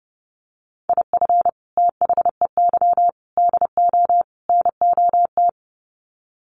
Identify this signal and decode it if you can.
Morse code, heard as a keyed tone, about 30 wpm, 700 Hz. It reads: IF THEY DO NOT
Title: Sets of 4 Words 30wpm